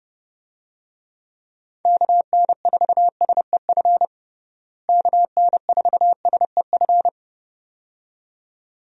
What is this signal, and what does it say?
Morse code, heard as a keyed tone, about 30 wpm, 700 Hz. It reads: KN4SEF KN4SEF
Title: USA Call Signs - 6 Characters - Repeated 2x - 30wpm